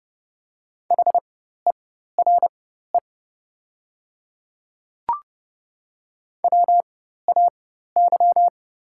Morse code, heard as a keyed tone, about 30 wpm, spaced as for 14 wpm, 700 Hz. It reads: HERE WAY
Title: 4x Character Spacing - Top 300 Words - 30wpm